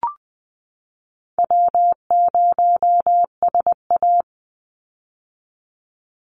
Morse code, heard as a keyed tone, about 20 wpm, 700 Hz. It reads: W0SA
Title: Call Signs - 20wpm